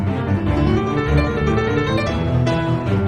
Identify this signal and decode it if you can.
There is no code.